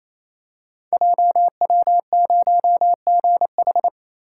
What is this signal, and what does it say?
Morse code, heard as a keyed tone, about 29 wpm, 700 Hz. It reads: JW0GH
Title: Code Groups - 5 Letter:Numbers - 28wpm